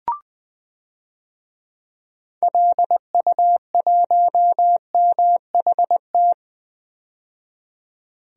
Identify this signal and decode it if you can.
Morse code, heard as a keyed tone, about 20 wpm, 700 Hz. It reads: LU1MHT
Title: DXCC-100 Call Signs - 6 Characters - 20wpm